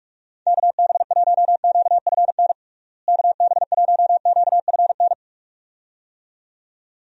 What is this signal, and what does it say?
Morse code, heard as a keyed tone, about 46 wpm, 700 Hz. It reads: KB1QFN KB1QFN
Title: USA Call Signs - 6 Characters - Repeated 2x - 45wpm